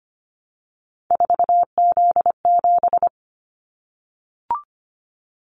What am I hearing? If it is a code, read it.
Morse code, heard as a keyed tone, about 25 wpm, 700 Hz. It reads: 4Z7
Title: Callsign Prefixes for Top 50 DXCC - 25wpm